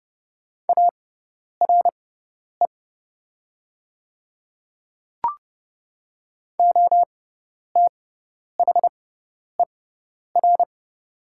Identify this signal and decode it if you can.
Morse code, heard as a keyed tone, about 30 wpm, spaced as for 10 wpm, 700 Hz. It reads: ARE OTHER
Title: 6x Character Spacing - Top 100 Words - 30wpm